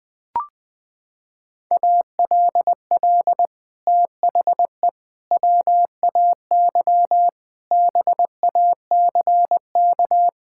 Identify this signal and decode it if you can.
Morse code, heard as a keyed tone, about 20 wpm, 700 Hz. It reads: ALL THE WAY BACK
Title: Sets of 6 Words - 20wpm